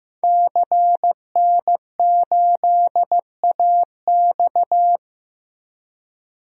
Morse code, heard as a keyed tone, about 15 wpm, 700 Hz. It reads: CN8AX
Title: DXCC-100 Call Signs - 5 Characters - Repeated 2x - 15wpm